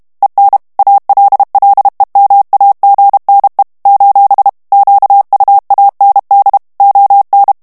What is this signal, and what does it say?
Morse code, heard as a keyed tone, about 32 wpm, 800 Hz. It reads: EN ALLEMAGNE : QUAND ON
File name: morse mp3 / FRANCAIS TEXTE CLAIR 1 VITESSE 30 MOTS MINUTES